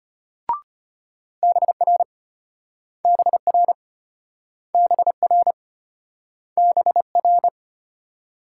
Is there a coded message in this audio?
Yes: Morse code from 1.4 to 7.5 s, heard as a keyed tone, about 34 wpm, 700 Hz. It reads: BR BR BR BR